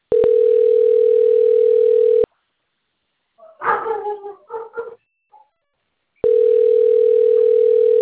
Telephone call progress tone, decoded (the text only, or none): ringback tone